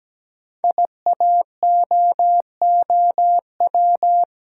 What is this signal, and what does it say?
Morse code, heard as a keyed tone, about 17 wpm, 700 Hz. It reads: IAOOW
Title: Code Groups - 5 Letters - 17wpm